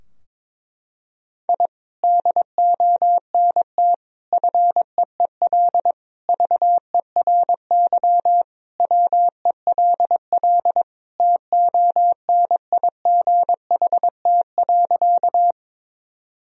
Morse code, heard as a keyed tone, about 22 wpm, 700 Hz. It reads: I DONT FEEL VERY WELL TONIGHT.
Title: Sentences from Top 1000 Words - Review Part 3 - 22wpm